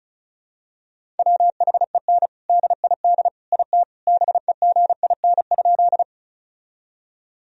Morse code, heard as a keyed tone, about 34 wpm, 700 Hz. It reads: WHEN DID IT BEGIN?
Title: Sentences from Top 1000 Words - Review Part 1 - 35wpm